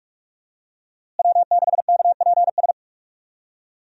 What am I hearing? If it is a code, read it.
Morse code, heard as a keyed tone, about 45 wpm, 700 Hz. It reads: W6KPS
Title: USA Call Signs - 5 Characters - Repeated 2x - 45wpm